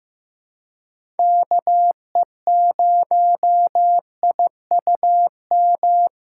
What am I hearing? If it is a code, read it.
Morse code, heard as a keyed tone, about 15 wpm, 700 Hz. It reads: KE0IUM
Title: USA Call Signs - 3x Character Spacing - 15wpm